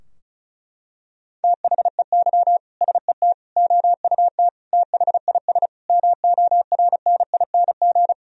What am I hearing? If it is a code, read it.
Morse code, heard as a keyed tone, about 36 wpm, 700 Hz. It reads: THEY SET OUT THIS MORNING
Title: Sentences from Top 1000 Words - Review Part 2 - 35wpm